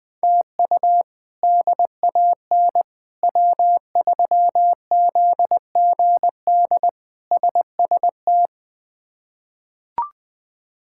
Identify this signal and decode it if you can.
Morse code, heard as a keyed tone, about 21 wpm, 700 Hz. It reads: TU DAN W3ZGD SST
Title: K1USN SST QSO Practice - 20wpm